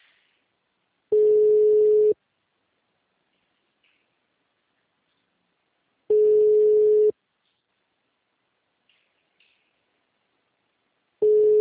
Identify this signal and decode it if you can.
Ringback tone (Europe, 425 Hz).